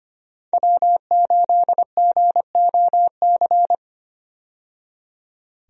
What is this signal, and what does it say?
Morse code, heard as a keyed tone, about 25 wpm, 700 Hz. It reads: W8GOC
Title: USA Call Signs - 5 Characters - 25wpm